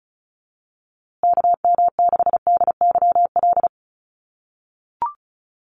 Morse code, heard as a keyed tone, about 34 wpm, 700 Hz. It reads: KM6DYL
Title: USA Call Signs Repeated 3x 35wpm